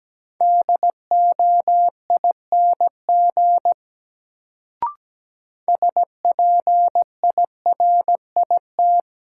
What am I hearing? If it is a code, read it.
Morse code, heard as a keyed tone, about 17 wpm, 700 Hz. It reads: DOING SPIRIT
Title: Sentences from Top 1400 Words - New Word Drill - 17wpm